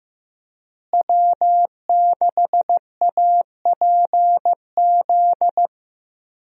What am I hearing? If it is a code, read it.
Morse code, heard as a keyed tone, about 15 wpm, 700 Hz. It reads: W6APZ
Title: USA Call Signs - 5 Characters - Repeated 3x - 15wpm